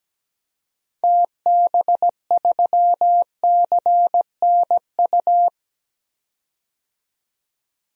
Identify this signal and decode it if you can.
Morse code, heard as a keyed tone, about 17 wpm, 700 Hz. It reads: TB3CNU
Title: DXCC-100 Call Signs - 6 Characters - 17wpm